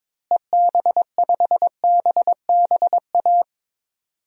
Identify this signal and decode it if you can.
Morse code, heard as a keyed tone, about 22 wpm, 700 Hz. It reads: EB5BBA
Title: DXCC-100 Call Signs - 6 Characters - Repeated 3x - 22wpm